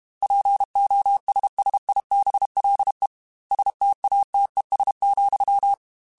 Morse code, heard as a keyed tone, about 32 wpm, 800 Hz. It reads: POSSIBLE STATES,
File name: Technical Subjects / Technical Subjects-mp3s / AND Gate @30